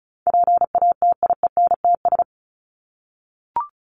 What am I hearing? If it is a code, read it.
Morse code, heard as a keyed tone, about 34 wpm, 700 Hz. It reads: PATIENTS
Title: ES:S Suffix - Top 100 Words with Base Words 35wpm